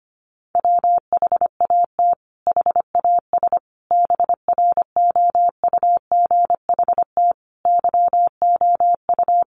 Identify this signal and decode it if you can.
Morse code, heard as a keyed tone, about 25 wpm, 700 Hz. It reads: WHAT HAS BROUGHT YOU
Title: Sentences from Top 500 Words 25wpm